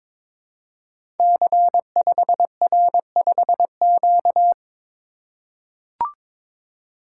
Morse code, heard as a keyed tone, about 22 wpm, 700 Hz. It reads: C5R5Q
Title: Code Groups - 5 Letter:Numbers - 22wpm